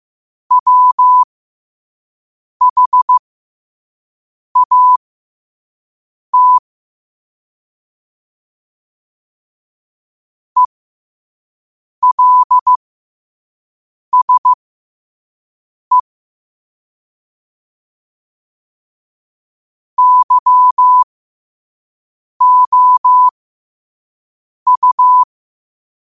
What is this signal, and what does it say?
Morse code, heard as a keyed tone, about 15 wpm, spaced as for 5 wpm, 1000 Hz. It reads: WHAT ELSE YOU
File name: qotdfiles / 5wpm